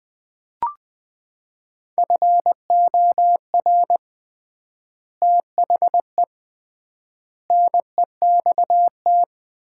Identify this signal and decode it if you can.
Morse code, heard as a keyed tone, about 20 wpm, 700 Hz. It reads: FOR THE NEXT
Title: 3x Word-Spacing - Sets of 3 Words - 20wpm